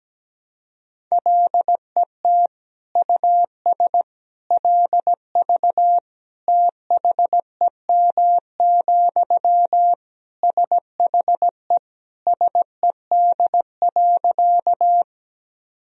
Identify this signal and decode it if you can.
Morse code, heard as a keyed tone, about 17 wpm, 700 Hz. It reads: LET US LV THEM, SHE SED.
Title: Sentences from Top 400 Words and Abbreviations 17wpm